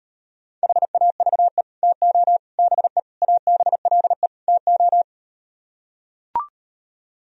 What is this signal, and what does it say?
Morse code, heard as a keyed tone, about 38 wpm, 700 Hz. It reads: HAVE TO BE ABLE TO